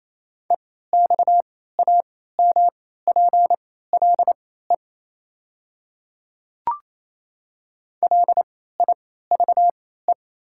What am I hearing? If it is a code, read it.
Morse code, heard as a keyed tone, about 28 wpm, spaced as for 16 wpm, 700 Hz. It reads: EXAMPLE LIVE
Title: 3x Character Spacing - Top 300 Words - 28wpm